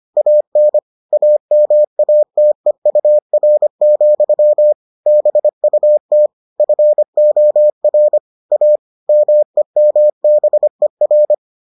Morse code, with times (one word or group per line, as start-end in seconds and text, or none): 0.2-0.8 AN
1.1-4.7 AMATEUR,
5.1-6.3 BUT
6.6-8.2 FOR
8.5-8.8 A
9.1-11.3 MEMBER